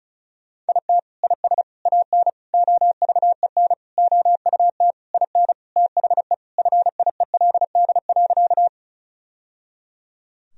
Morse code, heard as a keyed tone, about 36 wpm, 700 Hz. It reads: IT IS AN OVEN OUT IN THE FIELD.
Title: Sentences from Top 1000 Words - Review Part 4 - 35wpm